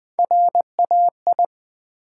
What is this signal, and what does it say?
Morse code, heard as a keyed tone, about 20 wpm, 700 Hz. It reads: RAI
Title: Build A Word - 1 Letter At A Time - 4-Letter Words - 20wpm